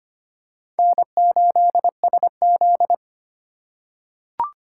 Morse code, heard as a keyed tone, about 25 wpm, 700 Hz. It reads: N8SZ
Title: USA Call Signs - 4 Characters - 3x Character Spacing - 25wpm